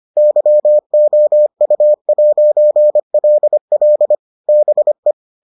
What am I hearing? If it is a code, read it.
Morse code, heard as a keyed tone, about 25 wpm, 600 Hz. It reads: YOU'LL BE